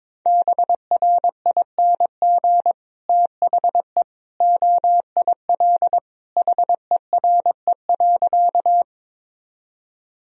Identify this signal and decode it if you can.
Morse code, heard as a keyed tone, about 22 wpm, 700 Hz. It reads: BRING THE OIL HERE.
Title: Sentences from Top 1000 Words - Review Part 4 - 22wpm